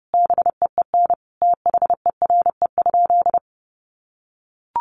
Morse code, heard as a keyed tone, about 30 wpm, 700 Hz. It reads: BEEN THERE?
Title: Sentences from Top 100 Words 30wpm